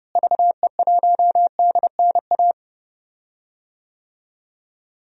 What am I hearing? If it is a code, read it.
Morse code, heard as a keyed tone, about 32 wpm, 700 Hz. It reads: VE1DNA
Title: DXCC-100 Call Signs - 6 Characters - Repeated 3x - 30wpm